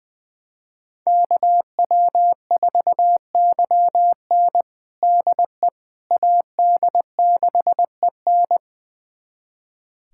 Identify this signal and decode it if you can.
Morse code, heard as a keyed tone, about 21 wpm, 700 Hz. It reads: KW4YN DE AD6EN
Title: USA Callsign DE Callsign - 5 Characters - 20wpm